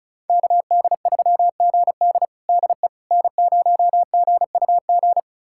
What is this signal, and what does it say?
Morse code, heard as a keyed tone, about 36 wpm, 690 Hz. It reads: KD3GD DE N0GUG
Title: USA Callsign DE Callsign - 5 Characters - 35wpm